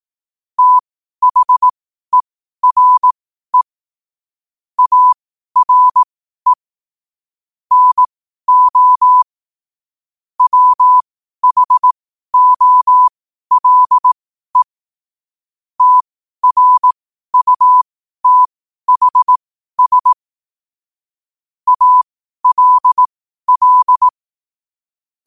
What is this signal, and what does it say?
Morse code, heard as a keyed tone, about 18 wpm, spaced as for 12 wpm, 1000 Hz. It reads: THERE ARE NO WHOLE TRUTHS ALL
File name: qotdfiles / 12wpm